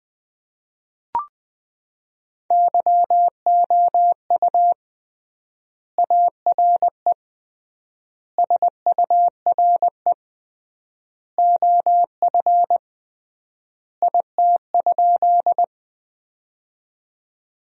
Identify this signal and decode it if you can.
Morse code, heard as a keyed tone, about 20 wpm, 700 Hz. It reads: YOU ARE SURE OF IT?
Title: 3x Word-Spacing - Sentences from Top 300 Words - 20wpm